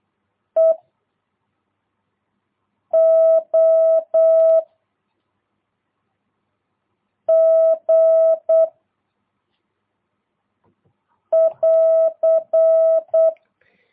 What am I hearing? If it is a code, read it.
Morse code, heard as a keyed tone, about 8 wpm, 660 Hz. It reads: E O G <AR>
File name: jh8cbh / 2022cw